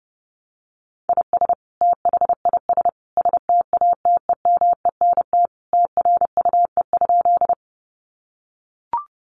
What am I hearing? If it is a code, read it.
Morse code, heard as a keyed tone, about 28 wpm, 700 Hz. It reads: IS THIS STATEMENT TRUE?
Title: Sentences from Top 1200 Words 30wpm